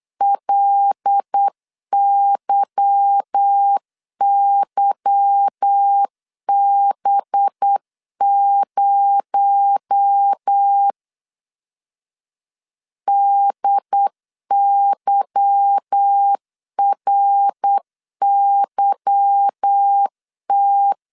Morse code, heard as a keyed tone, about 8 wpm, 800 Hz. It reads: LYYB0 DYRYT
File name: LEZIONI / 8 YJ45 / 8 Y 8wpm 3245